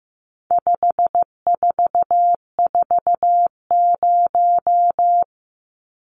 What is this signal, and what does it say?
Morse code, heard as a keyed tone, about 15 wpm, 700 Hz. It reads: 5440
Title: Numbers - 4-Digit 15wpm